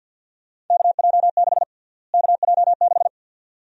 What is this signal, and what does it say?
Morse code, heard as a keyed tone, about 50 wpm, 700 Hz. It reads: KJ6 KJ6
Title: USA Call Sign Prefixes - Repeated 2x - 50wpm